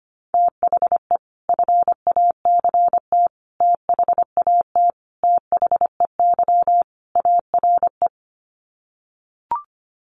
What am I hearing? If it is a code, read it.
Morse code, heard as a keyed tone, about 25 wpm, 700 Hz. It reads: THE FACT THAT THEY ARE
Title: Sets of 5 Words Encore2 25wpm